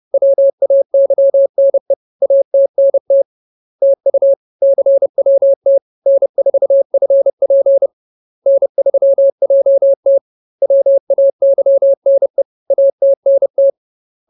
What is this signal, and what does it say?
Morse code, heard as a keyed tone, about 30 wpm, 550 Hz. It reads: WAYNE ATNT TU CWT N4FP N3JT WAYNE ATNT